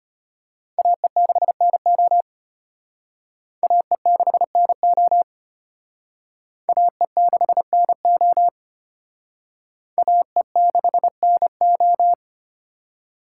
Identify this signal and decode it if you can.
Morse code, heard as a keyed tone, about 33 wpm, 700 Hz. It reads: AE6NO AE6NO AE6NO AE6NO